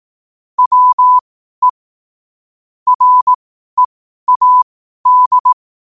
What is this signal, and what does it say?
Morse code, heard as a keyed tone, about 18 wpm, spaced as for 12 wpm, 1000 Hz. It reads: WE READ
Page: Quotes for Wed, 13 Aug 2025 in Morse Code at 12 words per minute.